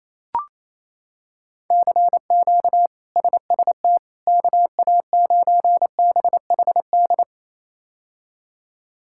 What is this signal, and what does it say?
Morse code, heard as a keyed tone, about 28 wpm, 700 Hz. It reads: CQ SST KA9BHD
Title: K1USN SST QSO Practice - 28wpm